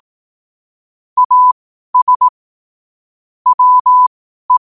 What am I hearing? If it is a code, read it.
Morse code, heard as a keyed tone, about 18 wpm, spaced as for 12 wpm, 1000 Hz. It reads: AS WE
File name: qotdfiles / 12wpm